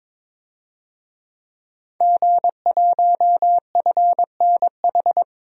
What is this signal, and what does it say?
Morse code, heard as a keyed tone, about 22 wpm, 700 Hz. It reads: G1FNH
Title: DXCC-100 Call Signs - 5 Characters - 22wpm